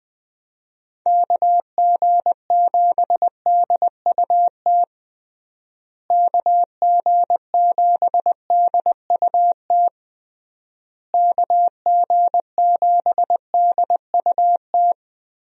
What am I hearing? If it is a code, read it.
Morse code, heard as a keyed tone, about 19 wpm, 700 Hz. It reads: KG7DUT KG7DUT KG7DUT
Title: USA Call Signs - 6 Characters - Repeated 3x - 20wpm